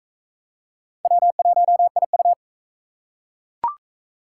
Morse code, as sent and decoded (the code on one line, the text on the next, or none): .-- .---- .. ..-
W1IU